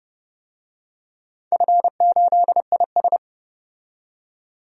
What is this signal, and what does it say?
Morse code, heard as a keyed tone, about 31 wpm, 700 Hz. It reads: F8IS
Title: DXCC-100 Call Signs - 4 Characters - 30wpm